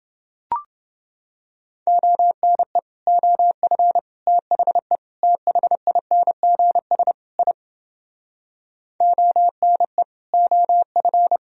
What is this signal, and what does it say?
Morse code, heard as a keyed tone, about 29 wpm, 700 Hz. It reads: ONE OF THE THINGS I ONE OF